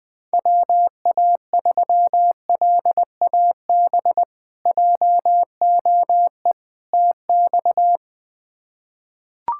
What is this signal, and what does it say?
Morse code, heard as a keyed tone, about 20 wpm, 700 Hz. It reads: WA3LAB JOE TX
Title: K1USN SST QSO Practice - 20wpm